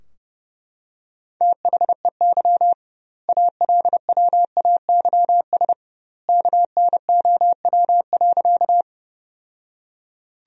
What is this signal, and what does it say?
Morse code, heard as a keyed tone, about 30 wpm, 700 Hz. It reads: THEY ALWAYS KNOW.
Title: 2x Word-Spacing - Sentences from Top 300 Words - 30wpm